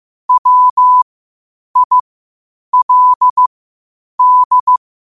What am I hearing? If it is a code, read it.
Morse code, heard as a keyed tone, about 15 wpm, spaced as for 8 wpm, 1000 Hz. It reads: WILD
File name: qotdfiles / 8wpm